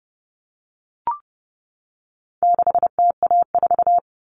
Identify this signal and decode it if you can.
Morse code, heard as a keyed tone, about 29 wpm, 700 Hz. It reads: 6TA4
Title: All Cut Numbers - 4-Digit 30wpm